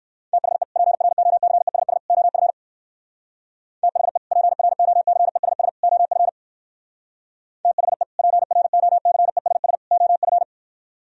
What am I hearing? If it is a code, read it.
Morse code, heard as a keyed tone, about 58 wpm, 700 Hz. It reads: THE PROCESS OF THE PROCESS OF THE PROCESS OF